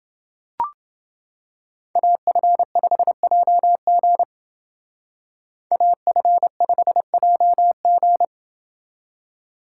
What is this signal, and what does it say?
Morse code, heard as a keyed tone, about 29 wpm, 700 Hz. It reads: AF5JG AF5JG